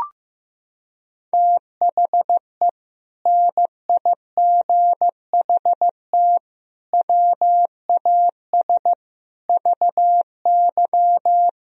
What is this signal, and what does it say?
Morse code, heard as a keyed tone, about 15 wpm, 700 Hz. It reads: THE NIGHT WAS VY
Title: Sentences from Top 500 Words and Abbreviations 15wpm